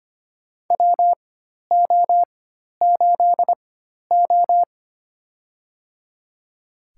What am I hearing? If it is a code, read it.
Morse code, heard as a keyed tone, about 25 wpm, spaced as for 12 wpm, 700 Hz. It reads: WO8O
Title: USA Call Signs - 4 Characters - 4x Character Spacing - 25wpm